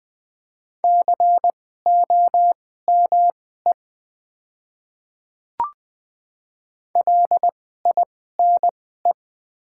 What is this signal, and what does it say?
Morse code, heard as a keyed tone, about 20 wpm, spaced as for 15 wpm, 700 Hz. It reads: COME LINE
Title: 2x Character Spacing - Top 200 Words - 20wpm